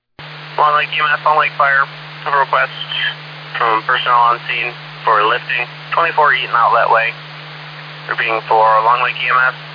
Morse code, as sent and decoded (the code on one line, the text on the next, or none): none